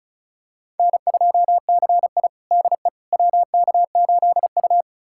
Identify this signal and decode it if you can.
Morse code, heard as a keyed tone, about 36 wpm, 700 Hz. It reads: N2CI DE WK8U
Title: USA Callsign DE Callsign - 4 Characters - 35wpm